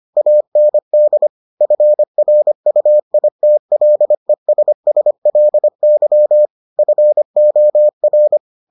Morse code, heard as a keyed tone, about 25 wpm, 600 Hz. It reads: AND FRUITLESSLY FOR